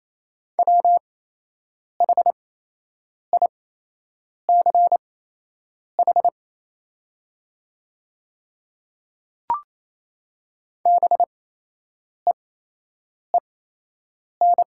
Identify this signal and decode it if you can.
Morse code, heard as a keyed tone, about 28 wpm, spaced as for 8 wpm, 700 Hz. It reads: WHICH BEEN
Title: 8x Character Spacing - Top 200 Words - 28wpm